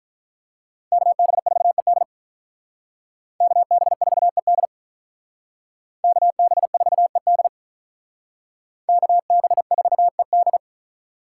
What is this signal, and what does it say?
Morse code, heard as a keyed tone, about 45 wpm, 700 Hz. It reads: KB4ED KB4ED KB4ED KB4ED